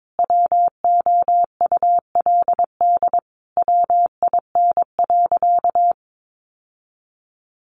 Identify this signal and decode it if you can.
Morse code, heard as a keyed tone, about 22 wpm, 700 Hz. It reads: WOULD WIN.
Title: Sentences from Top 1500 Words 22wpm